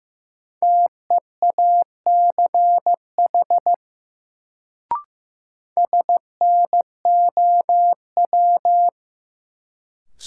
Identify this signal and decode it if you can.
Morse code, heard as a keyed tone, about 15 wpm, 700 Hz. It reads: TEACH SNOW
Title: Sentences from Top 500 Words - New Word Drill - 15wpm